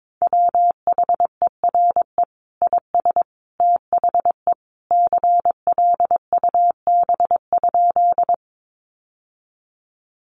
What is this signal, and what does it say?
Morse code, heard as a keyed tone, about 22 wpm, 700 Hz. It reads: WHERE IS THE CLUB?
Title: Sentences from Top 1200 Words 22wpm